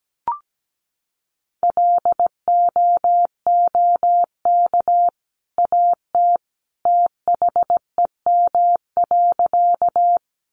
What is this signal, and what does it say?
Morse code, heard as a keyed tone, about 17 wpm, 700 Hz. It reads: LOOK AT THEM.
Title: Sentences from Top 100 Words 17wpm